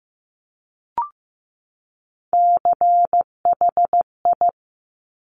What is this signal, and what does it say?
Morse code, heard as a keyed tone, about 15 wpm, 700 Hz. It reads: CHI
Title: Top Three Letter Combinations in Words 15wpm